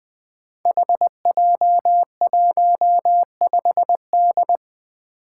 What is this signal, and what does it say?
Morse code, heard as a keyed tone, about 20 wpm, 700 Hz. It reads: HJ15D
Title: Five Challenging Letter-Number Combinations 20wpm